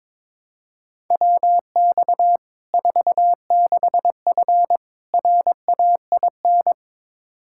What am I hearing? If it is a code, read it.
Morse code, heard as a keyed tone, about 21 wpm, 700 Hz. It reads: WX 46F RAIN
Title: Practice QSO - Exchange 2 - Info - 22wpm